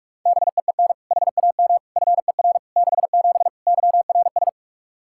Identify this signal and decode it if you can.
Morse code, heard as a keyed tone, about 46 wpm, 700 Hz. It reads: BEEN HAM FER 67 YRS
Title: Practice QSO - Exchange 3 - More Info - 45wpm